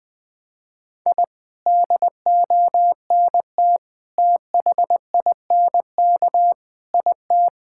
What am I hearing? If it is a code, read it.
Morse code, heard as a keyed tone, about 20 wpm, 700 Hz. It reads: I DONT THINK IT
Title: Sentences from Top 1000 Words - Review Part 6 - 20wpm